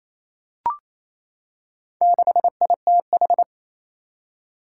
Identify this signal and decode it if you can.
Morse code, heard as a keyed tone, about 27 wpm, 700 Hz. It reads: 6ITH
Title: Four Challenging Letter-Number Combinations 28wpm